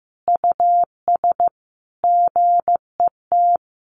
Morse code, heard as a keyed tone, about 15 wpm, 700 Hz. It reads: US GET
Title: A Sound Of Thunder 15wpm